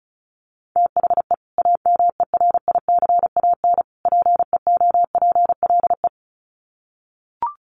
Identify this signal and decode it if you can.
Morse code, heard as a keyed tone, about 34 wpm, 700 Hz. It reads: THE AMERICAN PEOPLE
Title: Sets of 3 Words 35wpm